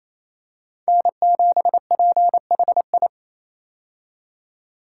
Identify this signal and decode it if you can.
Morse code, heard as a keyed tone, about 28 wpm, 700 Hz. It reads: N7PHI
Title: K1USN SST QSO Practice - 28wpm